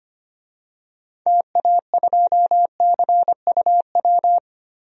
Morse code, heard as a keyed tone, about 25 wpm, 700 Hz. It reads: TA2CUW
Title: DXCC-100 Call Signs - 6 Characters - Repeated 2x - 25wpm